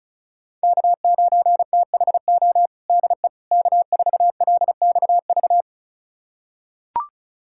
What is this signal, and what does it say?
Morse code, heard as a keyed tone, about 34 wpm, 700 Hz. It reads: K9THO DE K4LXV
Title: USA Callsign DE Callsign 35wpm